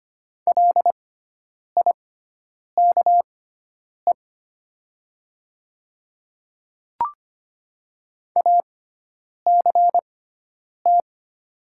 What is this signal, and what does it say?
Morse code, heard as a keyed tone, about 25 wpm, spaced as for 9 wpm, 700 Hz. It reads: LIKE ACT
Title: 6x Character Spacing - Top 200 Words - 25wpm